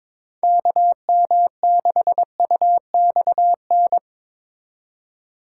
Morse code, heard as a keyed tone, about 21 wpm, 700 Hz. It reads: KM6UXN
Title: USA Call Signs - 6 Characters - 4x Character Spacing - 22wpm